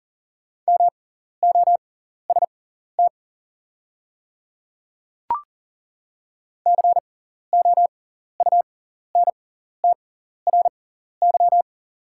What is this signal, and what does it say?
Morse code, heard as a keyed tone, about 40 wpm, spaced as for 14 wpm, 700 Hz. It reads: MOST COUNTRY
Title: 6x Character Spacing - Top 300 Words - 40wpm